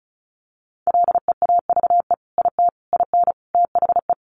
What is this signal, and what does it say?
Morse code, heard as a keyed tone, about 34 wpm, 700 Hz. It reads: LEAVE IT IN THE
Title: Sentences from Top 1300 Words 35wpm